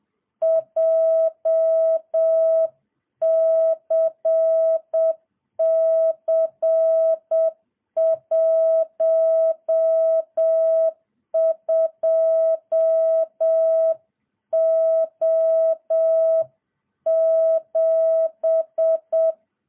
Morse code, heard as a keyed tone, about 7 wpm, 650 Hz. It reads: JCC12O7